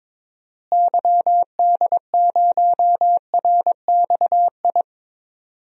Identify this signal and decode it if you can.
Morse code, heard as a keyed tone, about 22 wpm, 700 Hz. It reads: YD0RXI
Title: DXCC-100 Call Signs - 6 Characters - Repeated 2x - 22wpm